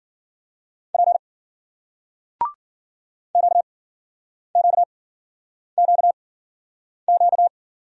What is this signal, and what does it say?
Morse code, heard as a keyed tone, about 54 wpm, spaced as for 9 wpm, 700 Hz. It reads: P QQQQ